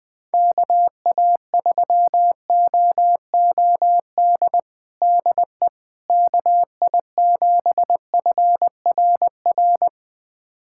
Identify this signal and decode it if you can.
Morse code, heard as a keyed tone, about 20 wpm, 690 Hz. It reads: KA3OOD DE KI7FRR
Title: USA Callsign DE Callsign 20wpm